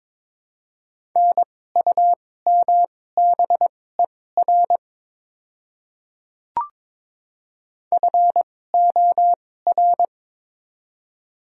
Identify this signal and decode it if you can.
Morse code, heard as a keyed tone, about 21 wpm, spaced as for 16 wpm, 700 Hz. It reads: NUMBER FOR
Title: 2x Character Spacing - Top 100 Words - 22wpm